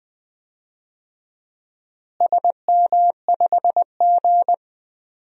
Morse code, heard as a keyed tone, about 20 wpm, 700 Hz. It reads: SM5G
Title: DXCC-100 Call Signs - 4 Characters - Repeated 2x - 20wpm